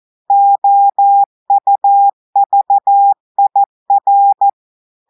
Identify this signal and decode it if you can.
Morse code, heard as a keyed tone, about 14 wpm, 800 Hz. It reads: OUVIR
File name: audios / portugais / portugais texte en clair 2 vitesse 14 mots minutes